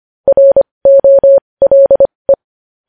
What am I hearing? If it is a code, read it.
Morse code, heard as a keyed tone, about 25 wpm, spaced as for 20 wpm, 550 Hz. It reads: ROLE
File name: podcast / News headlines-1x-25x20